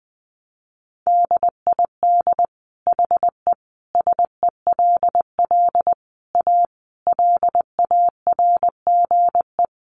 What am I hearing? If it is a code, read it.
Morse code, heard as a keyed tone, about 20 wpm, 700 Hz. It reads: DID HE SELL A LARGE
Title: Sentences from Top 1300 Words 20wpm